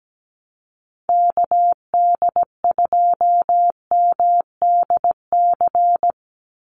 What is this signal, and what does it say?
Morse code, heard as a keyed tone, about 17 wpm, 700 Hz. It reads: KD2MDC
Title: USA Call Signs 17wpm